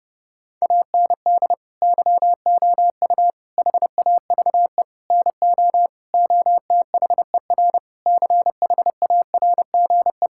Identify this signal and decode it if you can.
Morse code, heard as a keyed tone, about 30 wpm, 700 Hz. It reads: AND YOU HAVE NO OTHER CHARGE
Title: Sentences from Top 1000 Words - Review Part 6 - 30wpm